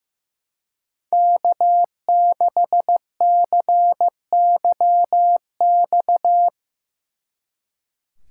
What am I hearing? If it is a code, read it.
Morse code, heard as a keyed tone, about 17 wpm, 700 Hz. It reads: K6CYX
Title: Speed Racing - USA 1x3 Call Signs 15wpm